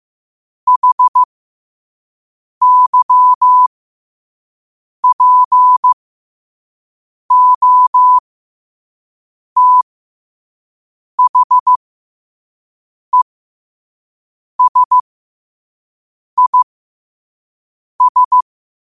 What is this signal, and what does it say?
Morse code, heard as a keyed tone, about 15 wpm, spaced as for 5 wpm, 1000 Hz. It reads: HYPOTHESIS
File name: qotdfiles / 5wpm